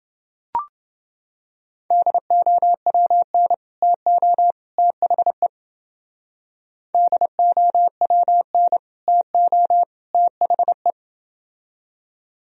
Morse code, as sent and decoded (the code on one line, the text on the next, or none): -.. --- .-- -. / - --- / - .... / -.. --- .-- -. / - --- / - .... .
DOWN TO THE DOWN TO THE